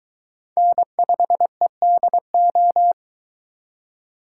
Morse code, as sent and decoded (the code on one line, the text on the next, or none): -. ..... -.. ---
N5EDO